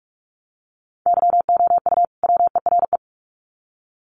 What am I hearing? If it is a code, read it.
Morse code, heard as a keyed tone, about 44 wpm, 700 Hz. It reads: YOU WERE
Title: Sets of 2 Words 45wpm